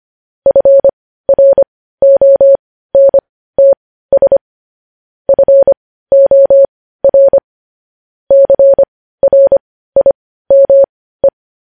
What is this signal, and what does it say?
Morse code, heard as a keyed tone, about 25 wpm, spaced as for 15 wpm, 550 Hz. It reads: FRONTS FOR CRIME
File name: podcast / News headlines-1x-25x15